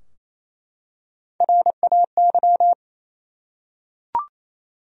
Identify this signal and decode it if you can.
Morse code, heard as a keyed tone, about 27 wpm, 700 Hz. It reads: RAY
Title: USA Common Names - Max 4 Letters - 28wpm